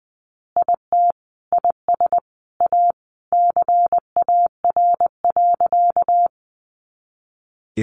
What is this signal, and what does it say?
Morse code, heard as a keyed tone, about 19 wpm, 700 Hz. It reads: IT IS A CAR.
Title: Easy Phrases 20wpm